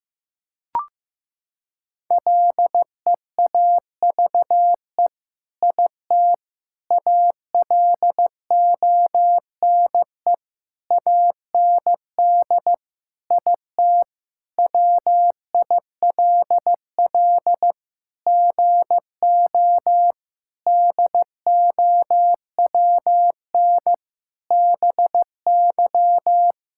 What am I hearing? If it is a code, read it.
Morse code, heard as a keyed tone, about 15 wpm, 700 Hz. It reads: LEAVE IT ALONE AND IT WILL GO DOWN BY
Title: Sentences from Top 1000 Words - Review Part 5 - 15wpm